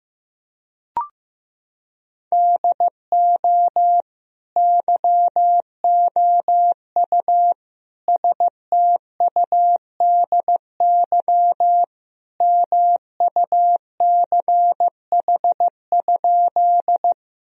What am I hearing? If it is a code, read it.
Morse code, heard as a keyed tone, about 15 wpm, 700 Hz. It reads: DO YOU STUDY MUCH?
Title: Sentences from Top 1000 Words - Review Part 1 - 15wpm